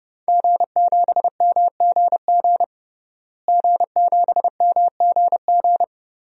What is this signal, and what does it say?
Morse code, heard as a keyed tone, about 30 wpm, 700 Hz. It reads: G7MGG G7MGG
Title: DXCC-100 Call Signs - 5 Characters - Repeated 2x - 30wpm